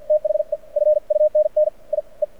Morse code, heard as a keyed tone, about 48 wpm, spaced as for 37 wpm, 610 Hz. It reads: THE FUNNIE
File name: archived / CRSnet